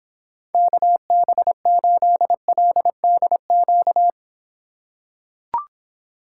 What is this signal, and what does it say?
Morse code, heard as a keyed tone, about 26 wpm, 700 Hz. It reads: KB8LDQ